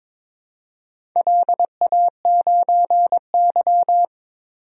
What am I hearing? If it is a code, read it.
Morse code, heard as a keyed tone, about 22 wpm, 700 Hz. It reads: LA9Y
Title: DXCC-100 Call Signs - 4 Characters - 22wpm